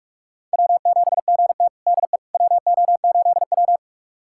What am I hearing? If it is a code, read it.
Morse code, heard as a keyed tone, about 46 wpm, 700 Hz. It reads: W7GT DE WO8W
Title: USA Callsign DE Callsign - 4 Characters - 45wpm